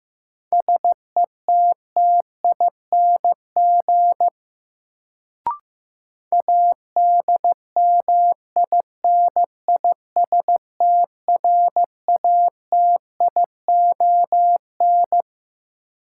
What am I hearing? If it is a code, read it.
Morse code, heard as a keyed tone, about 16 wpm, 700 Hz. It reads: SETTING ADMINISTRATION
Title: Sentences from Top 1300 Words - New Word Drill - 15wpm